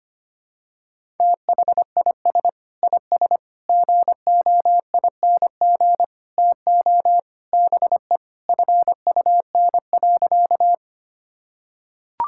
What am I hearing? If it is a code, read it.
Morse code, heard as a keyed tone, about 25 wpm, 700 Hz. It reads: THIS IS GOING TO BE FUN.
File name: practice / 25wpm / Sentences with 4-Word Phrases - 25wpm